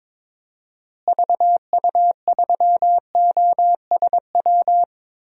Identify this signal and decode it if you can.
Morse code, heard as a keyed tone, about 22 wpm, 700 Hz. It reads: VU3OSW
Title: DXCC-100 Call Signs - 6 Characters - Repeated 2x - 22wpm